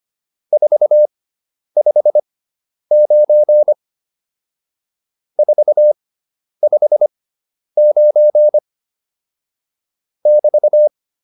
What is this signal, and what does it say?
Morse code, heard as a keyed tone, about 25 wpm, spaced as for 10 wpm, 600 Hz. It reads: 459 459 <BT>